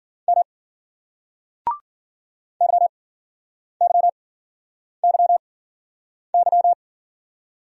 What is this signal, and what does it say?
Morse code, heard as a keyed tone, about 54 wpm, 700 Hz. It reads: M Y Y Y Y